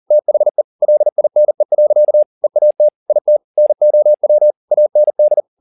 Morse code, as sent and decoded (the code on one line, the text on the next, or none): - .... / .-.. .. -. .-.-.- / . .- - / .. - / -. --- .-- / .- -. -..
THE LINE. EAT IT NOW AND